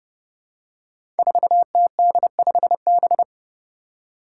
Morse code, heard as a keyed tone, about 30 wpm, 700 Hz. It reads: 4TD5B
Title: Five Challenging Letter-Number Combinations 30wpm